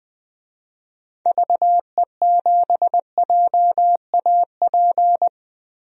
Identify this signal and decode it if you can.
Morse code, heard as a keyed tone, about 20 wpm, 700 Hz. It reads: VE7JAP
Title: DXCC-100 Call Signs - 6 Characters - 20wpm